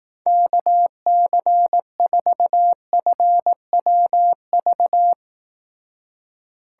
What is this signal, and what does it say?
Morse code, heard as a keyed tone, about 19 wpm, 700 Hz. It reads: KC4FWV
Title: Speed Racing - USA Call Signs 15wpm